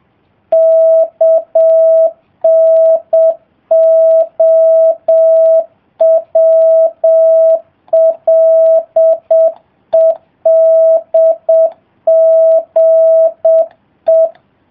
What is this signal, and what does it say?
Morse code, heard as a keyed tone, about 7 wpm, 650 Hz. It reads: KNOWLEDGE